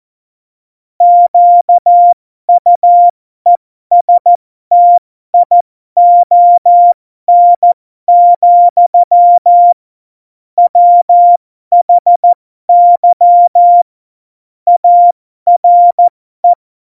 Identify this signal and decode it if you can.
Morse code, heard as a keyed tone, about 14 wpm, spaced as for 12 wpm, 700 Hz. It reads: QUESTION, WHY ARE
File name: Morsekurs / Witze